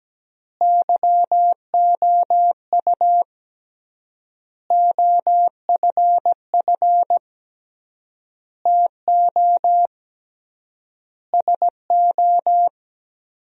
Morse code, heard as a keyed tone, about 17 wpm, 700 Hz. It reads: YOU OFF TO SO
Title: 3x Word-Spacing - Sentences from Top 400 Words - 17wpm